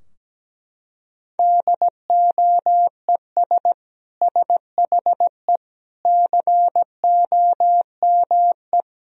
Morse code, heard as a keyed tone, about 17 wpm, 700 Hz. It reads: DOES SHE COME
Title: Sentences from Top 1000 Words - Review Part 4 - 17wpm